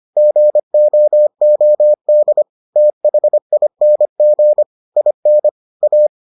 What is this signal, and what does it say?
Morse code, heard as a keyed tone, about 25 wpm, 600 Hz. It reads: GOOD THING IN A